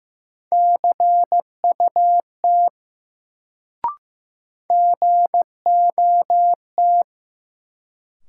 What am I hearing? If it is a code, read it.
Morse code, heard as a keyed tone, about 15 wpm, 700 Hz. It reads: CUT GOT
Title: Sentences from Top 300 Words - New Word Drill - 15wpm